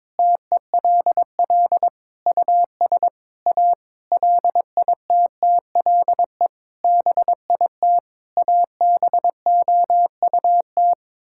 Morse code, heard as a keyed tone, about 21 wpm, 700 Hz. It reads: TELL US A LITTLE BIT ABOUT
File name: practice / 22wpm / Sets of 8 Words - 22wpm